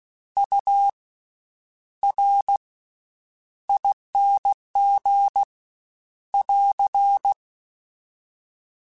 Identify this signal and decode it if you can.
Morse code, heard as a keyed tone, about 16 wpm, spaced as for 9 wpm, 800 Hz. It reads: URING <AR>
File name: Special INGs / Special INGs-mp3s / VerySlowToFaster Ing @15